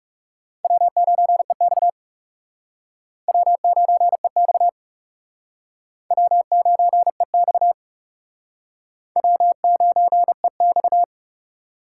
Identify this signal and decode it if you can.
Morse code, heard as a keyed tone, about 40 wpm, 700 Hz. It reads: W9EX W9EX W9EX W9EX